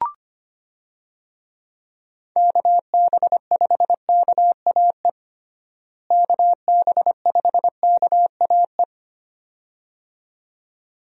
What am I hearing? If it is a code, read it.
Morse code, heard as a keyed tone, about 25 wpm, 700 Hz. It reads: KB5KAE KB5KAE
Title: USA Call Signs - 6 Characters - Repeated 2x - 25wpm